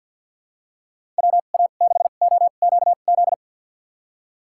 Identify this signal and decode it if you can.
Morse code, heard as a keyed tone, about 50 wpm, spaced as for 36 wpm, 700 Hz. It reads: WA6OQZ
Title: USA Call Signs - 2x Character Spacing - 50wpm